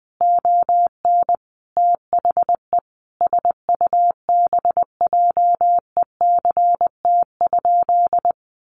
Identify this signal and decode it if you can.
Morse code, heard as a keyed tone, about 20 wpm, 700 Hz. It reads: ON THE SUBJECT?
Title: Sentences from Top 800 Words 20wpm